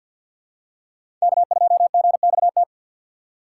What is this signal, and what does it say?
Morse code, heard as a keyed tone, about 48 wpm, 700 Hz. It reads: K2GXT
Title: USA Call Signs - 5 Characters - 6x Character Spacing - 50wpm